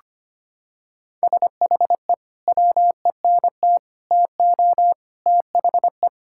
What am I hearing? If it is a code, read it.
Morse code, heard as a keyed tone, about 25 wpm, 700 Hz. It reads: SHE WENT TO THE
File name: practice / 25wpm / Sentences with 3-Word Phrases - 25wpm